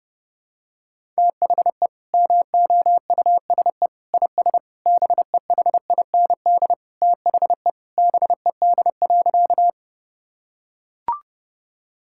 Morse code, heard as a keyed tone, about 30 wpm, 700 Hz. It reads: THE MOUSE IS BEHIND THE BED.
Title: Sentences from Top 1000 Words - Review Part 5 - 30wpm